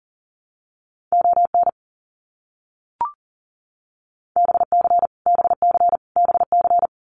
Morse code, heard as a keyed tone, about 40 wpm, 700 Hz. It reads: ON BC BC BC
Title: Canadian Provinces and Territories - Repeated 3x - 40wpm